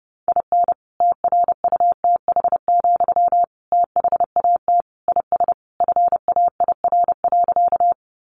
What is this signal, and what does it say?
Morse code, heard as a keyed tone, about 30 wpm, 700 Hz. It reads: IN TRUTH, THAT IS FAIR.
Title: Sentences from Top 1300 Words 30wpm